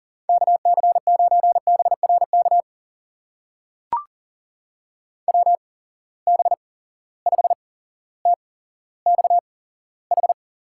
Morse code, heard as a keyed tone, about 42 wpm, spaced as for 17 wpm, 700 Hz. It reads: KC9BRK WB5TXH
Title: USA Call Signs - 6 Characters - 8x Character Spacing - 40wpm